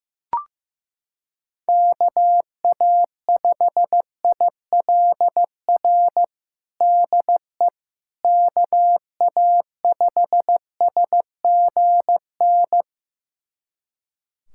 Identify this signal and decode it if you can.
Morse code, heard as a keyed tone, about 15 wpm, 700 Hz. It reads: KA5ILR DE KA5SGN
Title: USA Callsign DE Callsign - 6 Characters - 15wpm